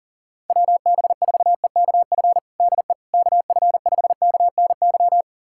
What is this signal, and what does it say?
Morse code, heard as a keyed tone, about 40 wpm, 700 Hz. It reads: WB4EKF DE KF5KNY
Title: USA Callsign DE Callsign - 6 Characters - 40wpm